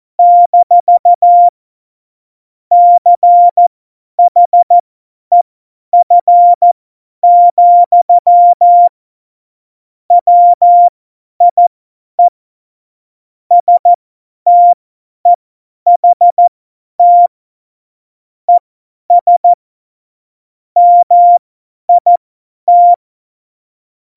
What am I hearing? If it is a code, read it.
Morse code, heard as a keyed tone, about 14 wpm, spaced as for 10 wpm, 700 Hz. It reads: - CHEF, WIE STEHT ES MIT
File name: Morsekurs / Witze / Witz 26. Februar 2019, 50 F70